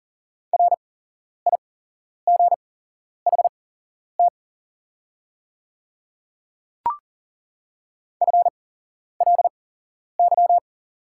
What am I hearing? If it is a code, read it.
Morse code, heard as a keyed tone, about 40 wpm, spaced as for 11 wpm, 700 Hz. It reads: RIGHT FLY
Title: 8x Character Spacing - Top 500 Words - 40wpm